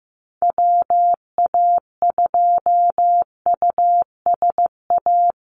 Morse code, heard as a keyed tone, about 15 wpm, 700 Hz. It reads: WA2USA
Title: NA SOTA Chaser Callsigns 15wpm